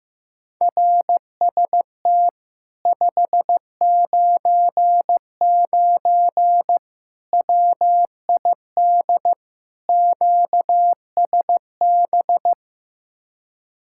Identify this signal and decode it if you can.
Morse code, heard as a keyed tone, about 16 wpm, 700 Hz. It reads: RST 599 WID QSB
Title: Speed-Racing Common QSO Phrases Encore - 15wpm